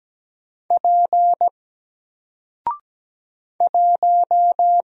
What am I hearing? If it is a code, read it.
Morse code, heard as a keyed tone, about 17 wpm, 700 Hz. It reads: P 1
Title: Single Character (Letter:Number:Punctuation:Prosign) Alt 17wpm